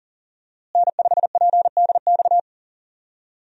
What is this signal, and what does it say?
Morse code, heard as a keyed tone, about 37 wpm, 700 Hz. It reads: N5PDX
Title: USA Call Signs - 5 Characters - Repeated 3x - 40wpm